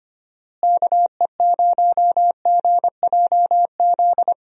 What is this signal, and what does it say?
Morse code, heard as a keyed tone, about 25 wpm, 700 Hz. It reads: KE0GJZ
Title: USA Call Signs - 6 Characters - Repeated 2x - 25wpm